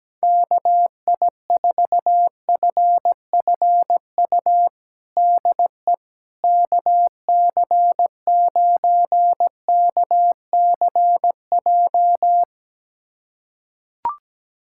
Morse code, heard as a keyed tone, about 17 wpm, 700 Hz. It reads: KI4FFU DE KC9KCJ
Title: USA Callsign DE Callsign 17wpm